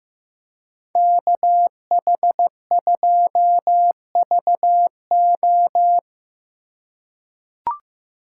Morse code, heard as a keyed tone, about 15 wpm, 700 Hz. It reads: KH2VO
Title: USA Call Signs - 5 Characters - 3x Character Spacing - 15wpm